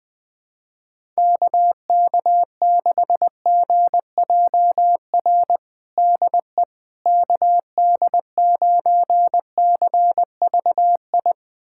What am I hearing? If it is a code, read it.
Morse code, heard as a keyed tone, about 20 wpm, 700 Hz. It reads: KK6GJR DE KD9CVI
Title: USA Callsign DE Callsign - 6 Characters - 20wpm